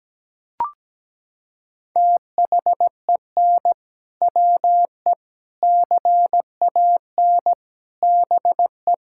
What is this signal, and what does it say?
Morse code, heard as a keyed tone, about 17 wpm, 700 Hz. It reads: THEN WE CAN BE
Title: Sentences from Top 1600 Words 17wpm